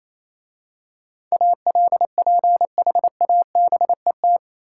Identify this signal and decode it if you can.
Morse code, heard as a keyed tone, about 27 wpm, 690 Hz. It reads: ALPHABET
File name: Easy Words - Speed-Racing with Hints - 22wpm